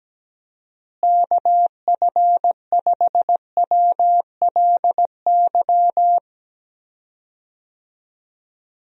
Morse code, heard as a keyed tone, about 17 wpm, 700 Hz. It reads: KF5WLY
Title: USA Call Signs - 6 Characters - Repeated 3x - 17wpm